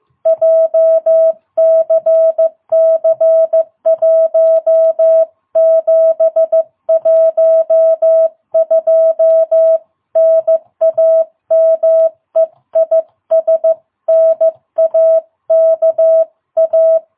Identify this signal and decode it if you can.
Morse code, heard as a keyed tone, about 15 wpm, 660 Hz. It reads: JCC1712 NAME IS NAKA
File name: jh8cbh / 2022cw